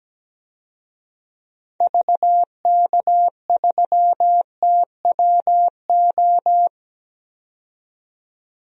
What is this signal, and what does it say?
Morse code, heard as a keyed tone, about 17 wpm, 700 Hz. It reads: VK3TWO
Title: DXCC-100 Call Signs - 6 Characters - 17wpm